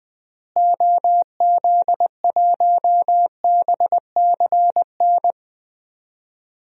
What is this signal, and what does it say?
Morse code, heard as a keyed tone, about 21 wpm, 700 Hz. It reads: OZ1BCN
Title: DXCC-100 Call Signs - 6 Characters - Repeated 3x - 20wpm